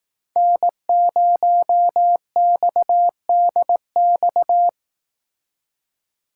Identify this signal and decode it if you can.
Morse code, heard as a keyed tone, about 18 wpm, 700 Hz. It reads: N0XDX
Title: Speed Racing - USA Call Signs 15wpm